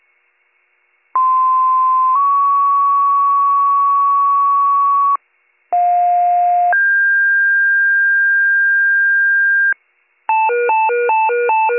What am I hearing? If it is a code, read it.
Two-tone page, heard as a one-second tone, then a three-second tone, about 1040 then 1120 Hz, about 710 then 1600 Hz. It is heard from 1.1 to 9.7 s.